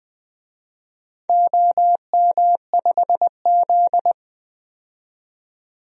Morse code, heard as a keyed tone, about 20 wpm, 700 Hz. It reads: OM5Z
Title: DXCC-100 Call Signs - 4 Characters - 20wpm